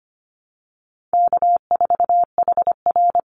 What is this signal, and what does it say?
Morse code, heard as a keyed tone, about 25 wpm, 700 Hz. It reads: K4HR
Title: Easy Call Signs Repeated 2x - 25wpm